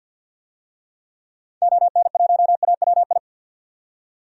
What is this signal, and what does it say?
Morse code, heard as a keyed tone, about 45 wpm, 700 Hz. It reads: ON1AWI
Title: DXCC-100 Call Signs - 6 Characters - 50wpm